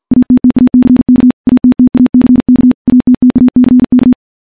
Morse code, heard as a keyed tone, about 51 wpm, 250 Hz. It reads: ATTACK ATTACK ATTACK
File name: morse / morse-training-50WPM